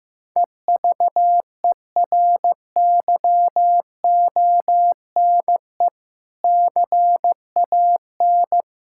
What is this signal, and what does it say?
Morse code, heard as a keyed tone, about 15 wpm, 700 Hz. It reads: EVERYONE CAN
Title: Sentences from Top 1600 Words 15wpm